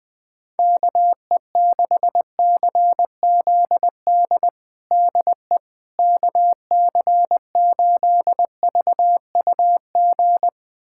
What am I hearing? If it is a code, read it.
Morse code, heard as a keyed tone, about 20 wpm, 700 Hz. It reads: KE6CZD DE KC8VUG
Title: USA Callsign DE Callsign - 6 Characters - 20wpm